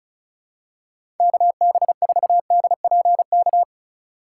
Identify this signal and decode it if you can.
Morse code, heard as a keyed tone, about 34 wpm, 700 Hz. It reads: KB4DPK
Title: USA Call Signs - 6 Characters - 3x Character Spacing - 35wpm